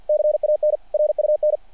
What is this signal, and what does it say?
Morse code, heard as a keyed tone, about 48 wpm, 600 Hz. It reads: CAN RUN
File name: archived / CRSnet